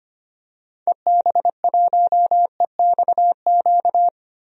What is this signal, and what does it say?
Morse code, heard as a keyed tone, about 25 wpm, 700 Hz. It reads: EB1EXQ
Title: DXCC-100 Call Signs - 6 Characters - 25wpm